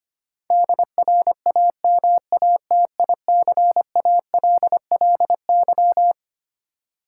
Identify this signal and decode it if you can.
Morse code, heard as a keyed tone, about 25 wpm, 700 Hz. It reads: DRAMATICALLY
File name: Twelve Letter Words - 25wpm